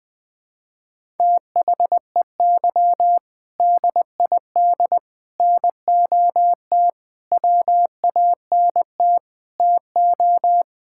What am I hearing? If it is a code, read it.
Morse code, heard as a keyed tone, about 21 wpm, 700 Hz. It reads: THEY DID NOT WANT TO
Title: Sentences with 5-Word Phrases - 20wpm